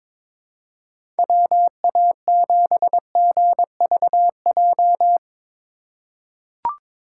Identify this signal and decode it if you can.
Morse code, heard as a keyed tone, about 22 wpm, 700 Hz. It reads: WA7GVJ
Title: USA Call Signs - 6 Characters - 3x Character Spacing - 22wpm